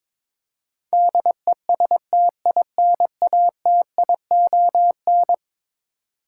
Morse code, heard as a keyed tone, about 22 wpm, 700 Hz. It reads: DESTINATION
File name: practice / 22wpm / Eleven Letter Words - 22wpm